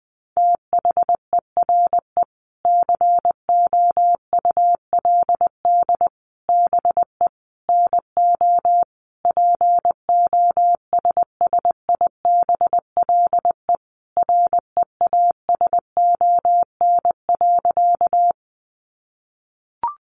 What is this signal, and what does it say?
Morse code, heard as a keyed tone, about 20 wpm, 700 Hz. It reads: THERE COULD BE NO POSSIBLE REASON.
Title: Sentences from Top 700 Words 20wpm